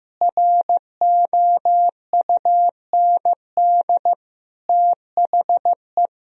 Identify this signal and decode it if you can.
Morse code, heard as a keyed tone, about 16 wpm, 700 Hz. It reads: ROUND THE
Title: Sentences from Top 1000 Words - Review Part 1 - 15wpm